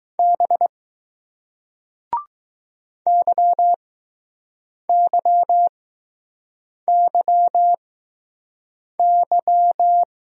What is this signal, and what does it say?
Morse code, heard as a keyed tone, about 20 wpm, spaced as for 6 wpm, 700 Hz. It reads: B YYYY